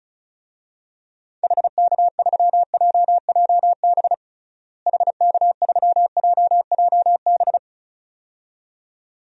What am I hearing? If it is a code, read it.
Morse code, heard as a keyed tone, about 34 wpm, 700 Hz. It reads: HK3JJB HK3JJB
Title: DXCC-100 Call Signs - 6 Characters - Repeated 2x - 35wpm